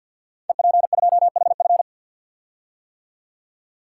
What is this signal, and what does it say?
Morse code, heard as a keyed tone, about 51 wpm, 700 Hz. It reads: EP2HF
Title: Code Groups - 5 Letter:Numbers - 50wpm